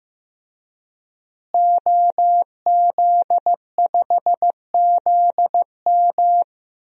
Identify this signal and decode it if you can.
Morse code, heard as a keyed tone, about 15 wpm, 700 Hz. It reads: OZ5ZM
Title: DXCC-100 Call Signs - 5 Characters - Repeated 3x - 15wpm